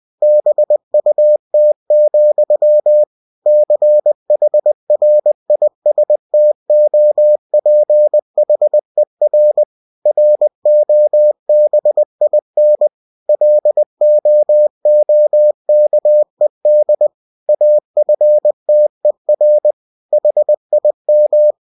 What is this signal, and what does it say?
Morse code, heard as a keyed tone, about 20 wpm, 600 Hz. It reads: BUT, CHRISTOPHER ROBIN LOOKED AFTER HIM
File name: cw-books / winnie-the-pooh / winnie-20wpm-page-0020